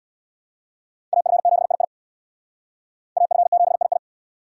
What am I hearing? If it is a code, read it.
Morse code, heard as a keyed tone, about 76 wpm, 700 Hz. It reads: AF7II AF7II